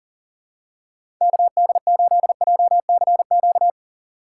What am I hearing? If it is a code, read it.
Morse code, heard as a keyed tone, about 40 wpm, 700 Hz. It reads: KD8JCQ
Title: USA Call Signs - 6 Characters - 4x Character Spacing - 40wpm